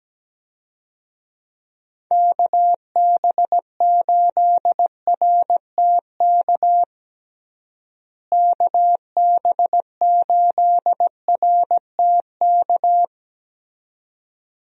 Morse code, heard as a keyed tone, about 17 wpm, 700 Hz. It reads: KB8RTK KB8RTK
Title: USA Call Signs - 6 Characters - Repeated 2x - 17wpm